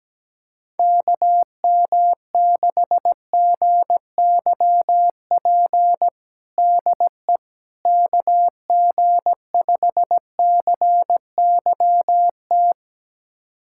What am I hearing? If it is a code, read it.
Morse code, heard as a keyed tone, about 17 wpm, 700 Hz. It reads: KM6GYP DE KG5CYT
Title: USA Callsign DE Callsign - 6 Characters - 17wpm